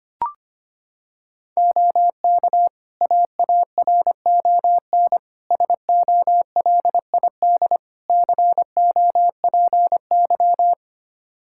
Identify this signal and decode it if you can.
Morse code, heard as a keyed tone, about 25 wpm, 700 Hz. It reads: OK AARON SOLID COPY
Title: K1USN SST QSO Practice 25wpm